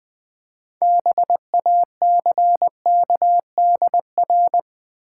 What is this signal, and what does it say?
Morse code, heard as a keyed tone, about 20 wpm, 700 Hz. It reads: BACKDR
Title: Build A Word - 3 Letters At A Time - 8-Letter Words - 20wpm